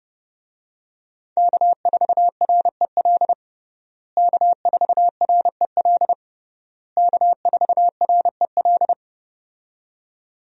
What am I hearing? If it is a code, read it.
Morse code, heard as a keyed tone, about 30 wpm, 700 Hz. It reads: K4REL K4REL K4REL
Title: USA Call Signs - 5 Characters - Repeated 3x - 30wpm